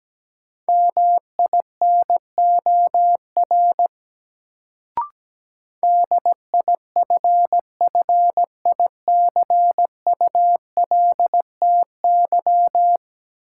Morse code, heard as a keyed tone, about 17 wpm, 700 Hz. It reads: MINOR DIFFICULTY
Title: Sentences from Top 1500 Words - New Word Drill - 17wpm